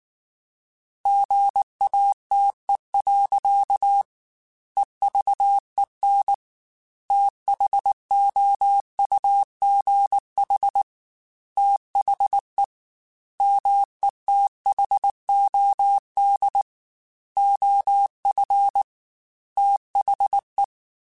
Morse code, heard as a keyed tone, about 19 wpm, 800 Hz. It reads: GATE. EVEN THOUGH THE METHOD OF THE
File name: Technical Subjects / Technical Subjects-mp3s / AND Gate @18